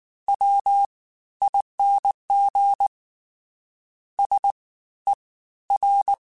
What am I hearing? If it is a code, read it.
Morse code, heard as a keyed tone, about 19 wpm, spaced as for 13 wpm, 800 Hz. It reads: WING SER
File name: Special INGs-mp3s / Special INGs-mp3s / Fast ing @18